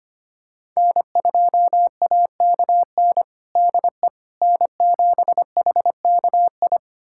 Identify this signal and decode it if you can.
Morse code, heard as a keyed tone, about 25 wpm, 700 Hz. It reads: N2AKN DE N7HKI
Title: USA Callsign DE Callsign - 5 Characters - 25wpm